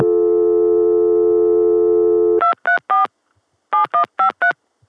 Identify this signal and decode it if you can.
DTMF, heard as telephone touch tones, tones of about 120 ms. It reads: 234 7153